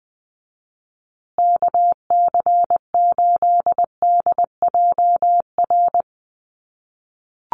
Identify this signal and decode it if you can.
Morse code, heard as a keyed tone, about 20 wpm, 700 Hz. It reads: KC8DJR
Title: USA Call Signs Repeated 2x 20wpm